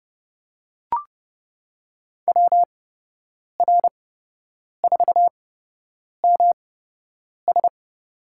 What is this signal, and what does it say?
Morse code, heard as a keyed tone, about 30 wpm, spaced as for 8 wpm, 700 Hz. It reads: WR4MS
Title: USA Call Signs - 5 Characters - 8x Character Spacing - 30wpm